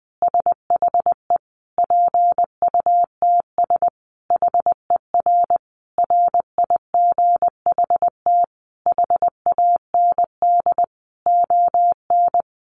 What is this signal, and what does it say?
Morse code, heard as a keyed tone, about 20 wpm, 700 Hz. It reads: SHE PUTS HER RIGHT HAND ON
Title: Easy Phrases 20wpm